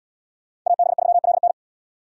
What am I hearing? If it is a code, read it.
Morse code, heard as a keyed tone, about 70 wpm, 700 Hz. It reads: AB3LN